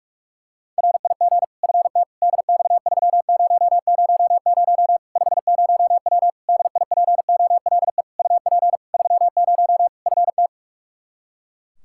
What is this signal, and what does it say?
Morse code, heard as a keyed tone, about 42 wpm, 700 Hz. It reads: RIG FT DX3000 50W DIPOLE UP 30 FT
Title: Practice QSO - Exchange 2 - Info - 45wpm